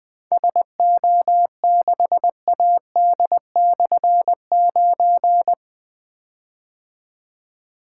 Morse code, heard as a keyed tone, about 20 wpm, 700 Hz. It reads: SO6AD/9
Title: Difficult Call Signs 20wpm